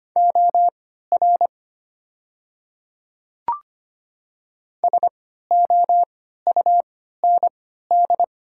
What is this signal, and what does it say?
Morse code, heard as a keyed tone, about 25 wpm, spaced as for 14 wpm, 700 Hz. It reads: OR SOUND
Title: 3x Character Spacing - Top 200 Words - 25wpm